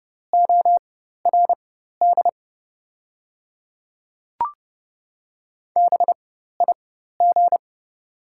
Morse code, heard as a keyed tone, about 30 wpm, spaced as for 14 wpm, 700 Hz. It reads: ORD BIG
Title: 4x Character Spacing - Top 500 Words - 30wpm